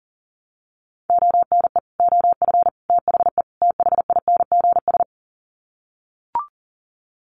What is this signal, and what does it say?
Morse code, heard as a keyed tone, about 37 wpm, 700 Hz. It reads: ONE OF THE THINGS
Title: Sets of 4 Words 40wpm